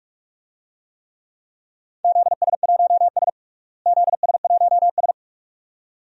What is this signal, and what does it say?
Morse code, heard as a keyed tone, about 46 wpm, 700 Hz. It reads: ZS1S ZS1S
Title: DXCC-100 Call Signs - 4 Characters - Repeated 2x - 45wpm